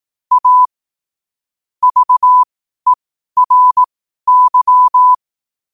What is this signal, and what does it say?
Morse code, heard as a keyed tone, about 18 wpm, spaced as for 12 wpm, 1000 Hz. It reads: A VERY
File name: qotdfiles / 12wpm